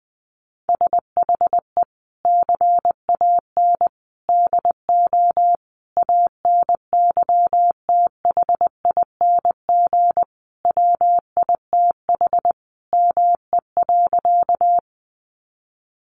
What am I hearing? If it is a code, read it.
Morse code, heard as a keyed tone, about 20 wpm, 700 Hz. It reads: SHE CAN DO ANYTHING WITH ME.
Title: Sentences from Top 1000 Words 20wpm